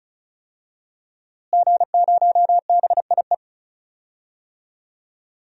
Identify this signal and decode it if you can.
Morse code, heard as a keyed tone, about 34 wpm, 700 Hz. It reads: G0BIE
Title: DXCC-100 Call Signs - 5 Characters - Repeated 3x - 35wpm